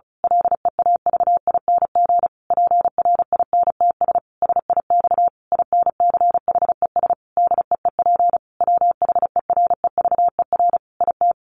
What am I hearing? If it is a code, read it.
Morse code, heard as a keyed tone, about 34 wpm, 700 Hz. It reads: LEAVING PRINTS SIX INCHES DEEP WHEREVER IT
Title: A Sound Of Thunder 35wpm